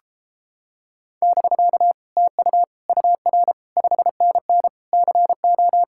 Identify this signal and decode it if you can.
Morse code, heard as a keyed tone, about 29 wpm, 700 Hz. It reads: <BK> TU UR 5NN CO